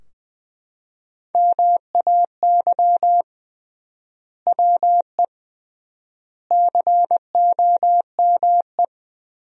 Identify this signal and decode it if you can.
Morse code, heard as a keyed tone, about 20 wpm, 700 Hz. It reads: MAY WE COME
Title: 3x Word-Spacing - Sentences from Top 100 Words - 20wpm